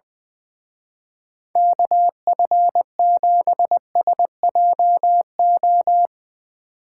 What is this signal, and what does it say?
Morse code, heard as a keyed tone, about 22 wpm, 700 Hz. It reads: KF7SJO
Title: Speed Racing - USA 2x3 Call Signs 20wpm